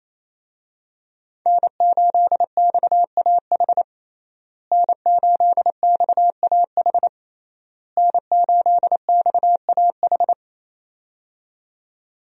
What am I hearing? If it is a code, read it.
Morse code, heard as a keyed tone, about 27 wpm, 700 Hz. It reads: N8XAH N8XAH N8XAH
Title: USA Call Signs - 5 Characters - Repeated 3x - 28wpm